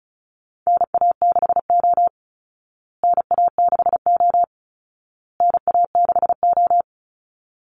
Morse code, heard as a keyed tone, about 34 wpm, 700 Hz. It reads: NA6O NA6O NA6O
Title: Call Signs Repeated 2x - 35wpm